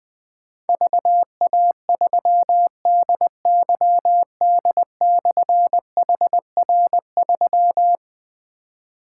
Morse code, heard as a keyed tone, about 20 wpm, 700 Hz. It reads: VA3DYD/HR3
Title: Difficult Call Signs 20wpm